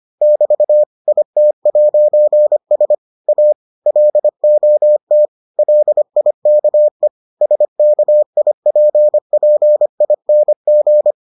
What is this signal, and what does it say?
Morse code, heard as a keyed tone, about 25 wpm, 600 Hz. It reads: <BT> IT'S A LOT LIKE SKIPPING